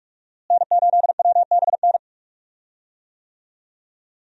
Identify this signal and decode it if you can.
Morse code, heard as a keyed tone, about 45 wpm, 700 Hz. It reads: N8WBN
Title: USA Call Signs - 5 Characters - Repeated 2x - 45wpm